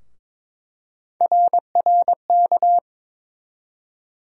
Morse code, heard as a keyed tone, about 22 wpm, 700 Hz. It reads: RRK
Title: USA Call Sign Postfixes - 3 Letters 22wpm